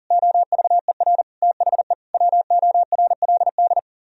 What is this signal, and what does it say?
Morse code, heard as a keyed tone, about 52 wpm, 700 Hz. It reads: OVER THE WORLD